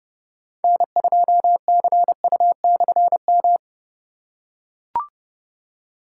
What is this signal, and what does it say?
Morse code, heard as a keyed tone, about 29 wpm, 700 Hz. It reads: N2CU/M
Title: K1USN SST QSO Practice - 30wpm